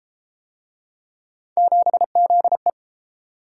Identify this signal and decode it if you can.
Morse code, heard as a keyed tone, about 31 wpm, 700 Hz. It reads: 7ZE